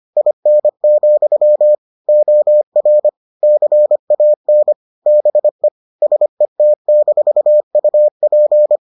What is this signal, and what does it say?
Morse code, heard as a keyed tone, about 25 wpm, 600 Hz. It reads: IN, OR CAN BE SET-UP